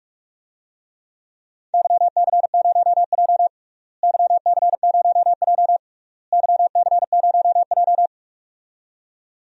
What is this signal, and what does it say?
Morse code, heard as a keyed tone, about 42 wpm, 700 Hz. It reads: YC0J YC0J YC0J
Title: DXCC-100 Call Signs - 4 Characters - Repeated 3x - 45wpm